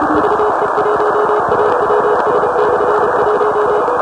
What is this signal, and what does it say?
Morse code, heard as a keyed tone, about 31 wpm, 420 Hz, with noise. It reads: VE1AWS/1
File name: WRTC1990-CW-Pileup